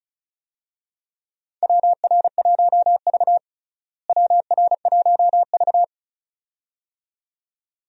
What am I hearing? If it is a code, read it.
Morse code, heard as a keyed tone, about 34 wpm, 700 Hz. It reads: WR1V WR1V
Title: USA Call Signs - 4 Characters - Repeated 2x - 35wpm